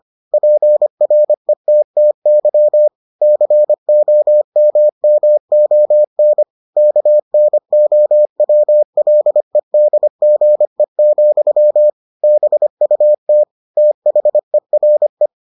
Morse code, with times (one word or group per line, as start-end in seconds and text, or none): 0.3-2.9 PRETTY
3.2-6.4 COMMON
6.8-11.9 KNOWLEDGE,
12.2-13.4 BUT
13.8-15.3 THERE